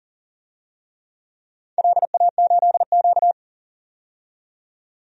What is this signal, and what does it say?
Morse code, heard as a keyed tone, about 42 wpm, 700 Hz. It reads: LA8Q
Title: DXCC-100 Call Signs - 4 Characters - 40wpm